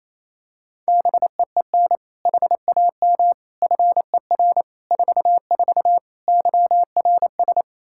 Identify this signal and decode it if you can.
Morse code, heard as a keyed tone, about 28 wpm, 700 Hz. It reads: BEEN HAM FER 44 YRS
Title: Practice QSO - Exchange 3 - More Info - 28wpm